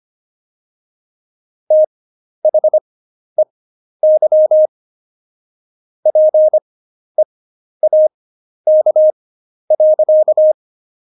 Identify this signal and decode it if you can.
Morse code, heard as a keyed tone, about 25 wpm, spaced as for 11 wpm, 620 Hz. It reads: THEY PEAK.
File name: Jokes-25-char-10-eff